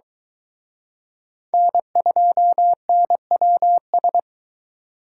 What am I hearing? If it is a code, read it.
Morse code, heard as a keyed tone, about 23 wpm, 700 Hz. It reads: N2NWS